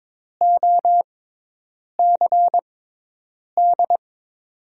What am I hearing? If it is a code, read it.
Morse code, heard as a keyed tone, about 22 wpm, spaced as for 8 wpm, 700 Hz. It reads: OCD
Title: USA Call Signs - 6 Characters - 6x Character Spacing - 22wpm